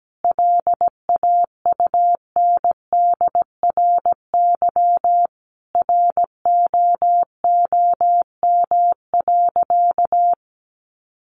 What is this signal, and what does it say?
Morse code, heard as a keyed tone, about 17 wpm, 700 Hz. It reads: LAUNDRY ROOM.
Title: Easy Phrases 17wpm